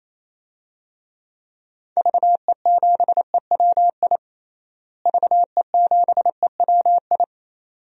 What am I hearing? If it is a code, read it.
Morse code, heard as a keyed tone, about 28 wpm, 700 Hz. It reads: VE7EWI VE7EWI
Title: DXCC-100 Call Signs - 6 Characters - Repeated 2x - 28wpm